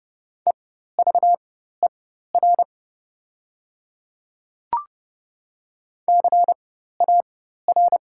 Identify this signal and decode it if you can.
Morse code, heard as a keyed tone, about 30 wpm, spaced as for 14 wpm, 700 Hz. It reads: EVER CAR
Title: 4x Character Spacing - Top 500 Words - 30wpm